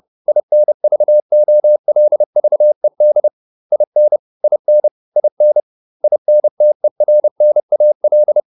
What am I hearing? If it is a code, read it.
Morse code, heard as a keyed tone, about 30 wpm, 600 Hz. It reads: INVOLVED IN IN IN INTERNAL